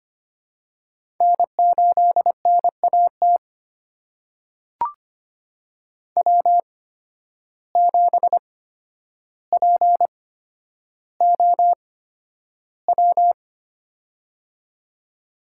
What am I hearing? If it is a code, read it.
Morse code, heard as a keyed tone, about 25 wpm, spaced as for 11 wpm, 700 Hz. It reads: N8NAT W7POW
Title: USA Call Signs - 8x Character Spacing - 25wpm